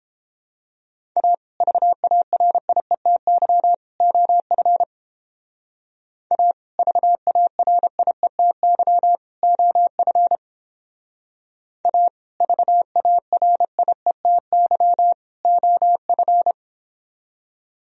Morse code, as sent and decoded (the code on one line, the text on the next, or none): .- / ...- .- .-. .. - -.-- / --- ..-. / .- / ...- .- .-. .. - -.-- / --- ..-. / .- / ...- .- .-. .. - -.-- / --- ..-.
A VARIETY OF A VARIETY OF A VARIETY OF